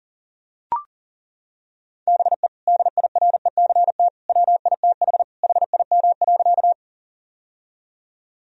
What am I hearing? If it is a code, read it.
Morse code, heard as a keyed tone, about 37 wpm, 700 Hz. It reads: BE DIRECT WITH HIM.
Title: Sentences from Top 1000 Words - Review Part 2 - 40wpm